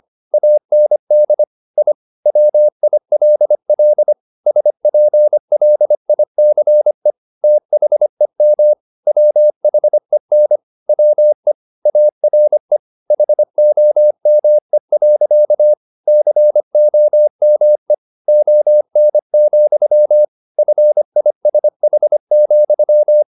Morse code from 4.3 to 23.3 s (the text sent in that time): SPLICE THEM WHEN WE ARE HOME. COME ON, FISH,